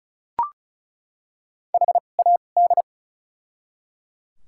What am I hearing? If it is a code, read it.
Morse code, heard as a keyed tone, about 36 wpm, spaced as for 25 wpm, 700 Hz. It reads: HAD
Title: 2x Character Spacing - Top 100 Words - 35wpm